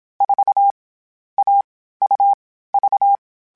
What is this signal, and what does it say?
Morse code, heard as a keyed tone, about 27 wpm, 800 Hz. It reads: 4 A U V
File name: CW Code Groups-mp3s / Sets mp3s / AUV AUV4 sp @25